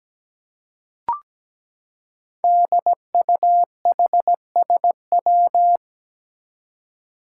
Morse code, heard as a keyed tone, about 17 wpm, 700 Hz. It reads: DUHSW
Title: Five Challenging Letter-Number Combinations 17wpm